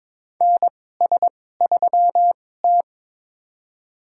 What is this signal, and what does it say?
Morse code, heard as a keyed tone, about 21 wpm, spaced as for 16 wpm, 700 Hz. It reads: NS3T
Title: USA Call Signs - 2x Character Spacing - 22wpm